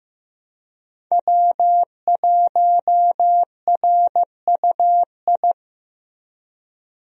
Morse code, heard as a keyed tone, about 15 wpm, 700 Hz. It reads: W1RUI
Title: USA Call Signs - 5 Characters - Repeated 2x - 15wpm